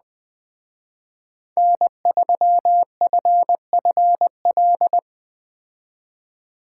Morse code, heard as a keyed tone, about 20 wpm, 700 Hz. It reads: N3FFL
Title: Speed Racing - USA 1x3 Call Signs 17wpm